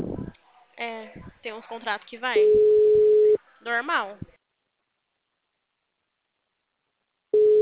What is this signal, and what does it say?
Ringback tone (Europe, 425 Hz), with speech.
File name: out- -2015- - - .94375